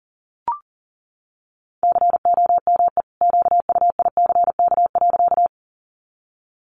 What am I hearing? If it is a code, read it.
Morse code, heard as a keyed tone, about 40 wpm, 700 Hz. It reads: COME QUICK.
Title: Sentences from Top 500 Words 40wpm